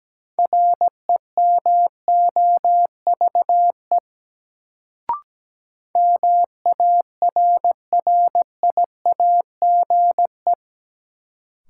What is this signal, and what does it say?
Morse code, heard as a keyed tone, about 17 wpm, 700 Hz. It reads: REMOVE MARRIAGE
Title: Sentences from Top 800 Words - New Word Drill - 17wpm